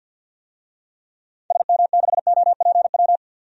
Morse code, heard as a keyed tone, about 53 wpm, 700 Hz. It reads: SM6OPW
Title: DXCC-100 Call Signs - 6 Characters - 50wpm